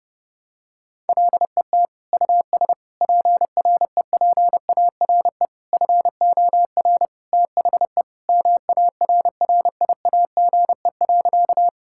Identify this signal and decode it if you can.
Morse code, heard as a keyed tone, about 32 wpm, 700 Hz. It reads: LET US PREPARE FOR THE MARRIAGE.
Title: Sentences from Top 1000 Words - Review Part 6 - 30wpm